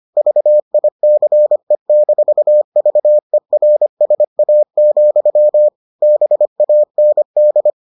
Morse code, heard as a keyed tone, about 25 wpm, 600 Hz. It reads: VICE-VERSA, BAND